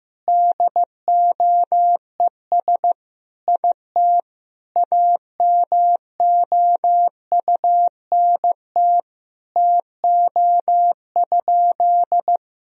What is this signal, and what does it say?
Morse code, heard as a keyed tone, about 15 wpm, 700 Hz. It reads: DOES IT AMOUNT TO?
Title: Sentences from Top 1000 Words - Review Part 4 - 15wpm